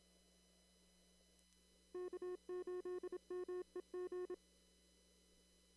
Morse code, heard as a keyed tone, about 27 wpm, 360 Hz. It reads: K8MEG
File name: CW / kt5j wpx cw-op-k5pi